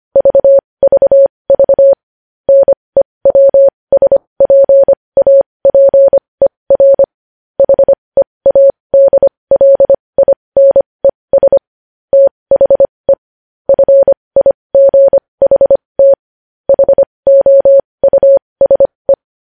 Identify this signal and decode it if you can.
Morse code, heard as a keyed tone, about 25 wpm, spaced as for 20 wpm, 550 Hz. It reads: VVV NEWSPAPER HEADLINES THE FIGHT HOUSE
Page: News Headlines in Morse Code at 20 WPM